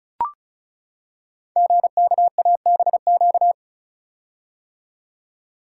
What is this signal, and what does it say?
Morse code, heard as a keyed tone, about 36 wpm, 690 Hz. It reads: GKABQ
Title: Code Groups - 5 Letters - 35wpm